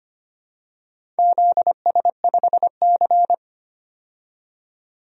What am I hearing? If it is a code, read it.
Morse code, heard as a keyed tone, about 25 wpm, 690 Hz. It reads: ZS5C
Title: DXCC-100 Call Signs - 4 Characters - 25wpm